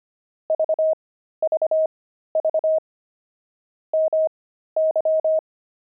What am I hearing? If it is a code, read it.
Morse code, heard as a keyed tone, about 25 wpm, spaced as for 13 wpm, 630 Hz. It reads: VVV MY
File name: pangram-fs-13-cs-25-grandpa